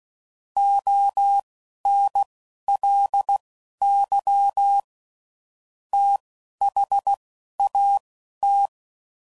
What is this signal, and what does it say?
Morse code, heard as a keyed tone, about 16 wpm, spaced as for 11 wpm, 800 Hz. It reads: ONLY THAT
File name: CW Code Groups @10WPM / Technical Subjects / AND Gate